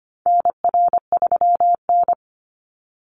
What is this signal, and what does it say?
Morse code, heard as a keyed tone, about 25 wpm, 700 Hz. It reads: NR3N
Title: Call Signs - 25wpm